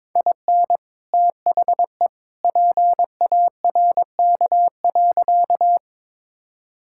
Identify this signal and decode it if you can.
Morse code, heard as a keyed tone, about 22 wpm, 700 Hz. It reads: IN THE PARK.
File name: practice / 22wpm / Sentences with 2-Word Phrases - 22wpm